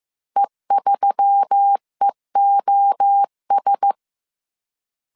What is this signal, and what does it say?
Morse code, heard as a keyed tone, about 15 wpm, 800 Hz. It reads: E3EOS
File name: CW / DET / 1 6 1500C S B 14wpm 2958